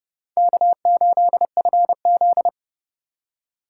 Morse code, heard as a keyed tone, about 29 wpm, 700 Hz. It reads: K8FZ